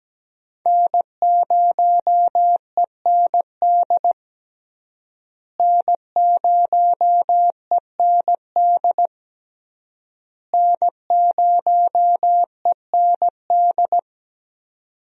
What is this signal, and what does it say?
Morse code, heard as a keyed tone, about 17 wpm, 700 Hz. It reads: N0END N0END N0END
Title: USA Call Signs - 5 Characters - Repeated 3x - 17wpm